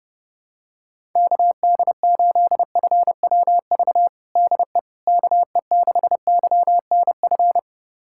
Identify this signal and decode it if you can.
Morse code, heard as a keyed tone, about 30 wpm, 700 Hz. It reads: KD8FWV DE KE6YNF
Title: USA Callsign DE Callsign 30wpm